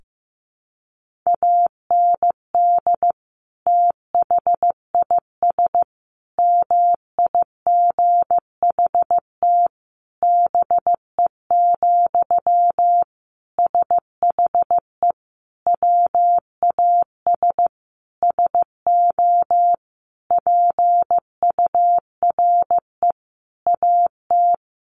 Morse code, heard as a keyed tone, about 15 wpm, 700 Hz. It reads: AND THIS MIGHT BE, SHE WAS SO PURE AT
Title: Sentences from Top 1500 Words 15wpm